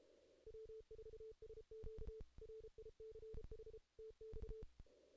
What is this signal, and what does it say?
Morse code, heard as a keyed tone, about 33 wpm, 440 Hz. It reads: W4SQ RICH TX
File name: 2016 JAN CW NAQP / 2016 Jan naqp k5ot - 0-11